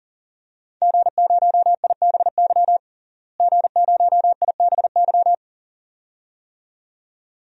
Morse code, heard as a keyed tone, about 37 wpm, 700 Hz. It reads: G0IBY G0IBY
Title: DXCC-100 Call Signs - 5 Characters - Repeated 2x - 40wpm